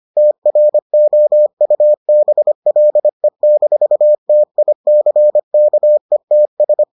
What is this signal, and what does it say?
Morse code, heard as a keyed tone, about 25 wpm, 600 Hz. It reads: TROUBLE-TICKETS